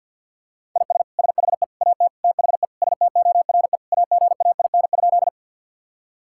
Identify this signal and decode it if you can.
Morse code, heard as a keyed tone, about 49 wpm, 700 Hz. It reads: IS SHE AT THE STORE AGAIN?
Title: Sentences from Top 1000 Words - Review Part 5 - 50wpm